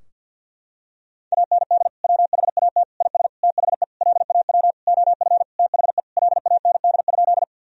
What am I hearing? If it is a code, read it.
Morse code, heard as a keyed tone, about 51 wpm, 700 Hz. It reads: AND WHAT IS THE LAW OF THE LAND?
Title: Sentences from Top 1000 Words - Review Part 3 - 50wpm